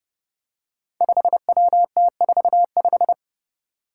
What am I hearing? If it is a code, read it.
Morse code, heard as a keyed tone, about 28 wpm, 700 Hz. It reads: 5WT45
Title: Five Challenging Letter-Number Combinations 30wpm